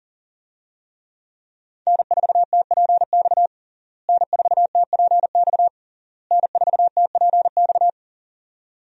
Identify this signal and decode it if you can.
Morse code, heard as a keyed tone, about 40 wpm, 700 Hz. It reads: N4TPX N4TPX N4TPX
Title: USA Call Signs - 5 Characters - Repeated 3x - 40wpm